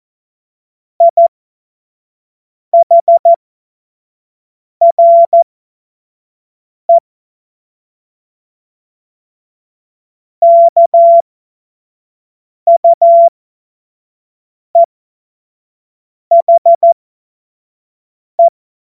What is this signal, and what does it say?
Morse code, heard as a keyed tone, about 14 wpm, spaced as for 5 wpm, 680 Hz. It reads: IHRE KUEHE
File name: Witz-23-September-2025-25-F70